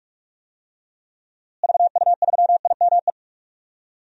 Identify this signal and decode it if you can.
Morse code, heard as a keyed tone, about 46 wpm, 700 Hz. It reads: VU3IME
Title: DXCC-100 Call Signs - 6 Characters - Repeated 3x - 45wpm